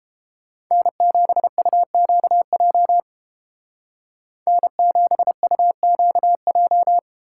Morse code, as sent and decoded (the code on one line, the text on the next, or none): -. --... ..- --.- .--- / -. --... ..- --.- .---
N7UQJ N7UQJ